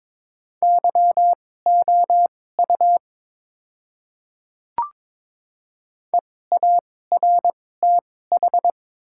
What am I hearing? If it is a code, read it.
Morse code, heard as a keyed tone, about 22 wpm, spaced as for 16 wpm, 700 Hz. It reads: YOU EARTH
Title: 2x Character Spacing - Top 200 Words - 22wpm